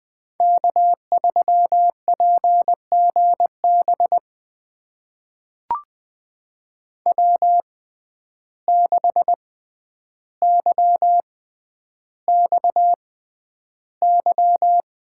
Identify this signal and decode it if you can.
Morse code, heard as a keyed tone, about 20 wpm, spaced as for 10 wpm, 700 Hz. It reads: K3PGB W6YXY
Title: USA Call Signs - 5 Characters - 6x Character Spacing - 20wpm